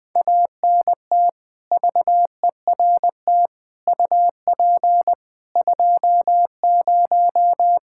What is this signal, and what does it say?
Morse code, heard as a keyed tone, about 21 wpm, 700 Hz. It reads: ANT VERT UP 20
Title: Practice QSO - Exchange 2 - Info - 20wpm